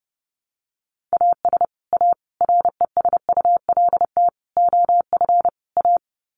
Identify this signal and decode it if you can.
Morse code, heard as a keyed tone, about 30 wpm, 700 Hz. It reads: AS A RESULT OF A
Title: Sets of 5 Words Encore 30wpm